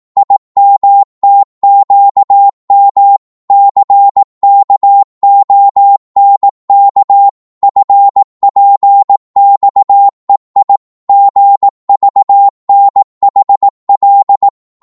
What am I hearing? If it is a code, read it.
Morse code, heard as a keyed tone, about 18 wpm, 800 Hz. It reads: IMTQM CKONK FPXEI GVNHL